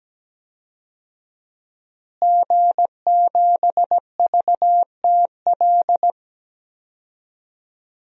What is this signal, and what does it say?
Morse code, heard as a keyed tone, about 17 wpm, 700 Hz. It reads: G7VTL
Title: DXCC-100 Call Signs - 5 Characters - Repeated 3x - 17wpm